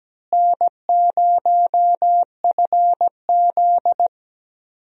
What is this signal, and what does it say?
Morse code, heard as a keyed tone, about 17 wpm, 700 Hz. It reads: N0FZ
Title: USA Call Signs - 4 Characters - Repeated 2x - 17wpm